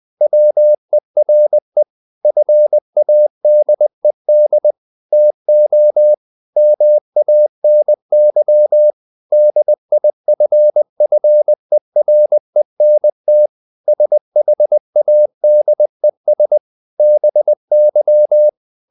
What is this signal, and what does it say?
Morse code, heard as a keyed tone, about 20 wpm, 600 Hz. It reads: WERE FADED TO MANY DIFFERENT SHADES BY
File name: the old man and the sea / the old man and the sea part 1 20wpm